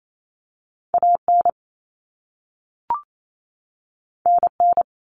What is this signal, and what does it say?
Morse code, heard as a keyed tone, about 28 wpm, 700 Hz. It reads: AN NN
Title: Common Cut Numbers - 2-Digit 28wpm